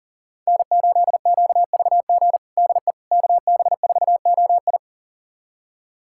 Morse code, heard as a keyed tone, about 40 wpm, 700 Hz. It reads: N8QVG DE KB4OI
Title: USA Callsign DE Callsign - 5 Characters - 40wpm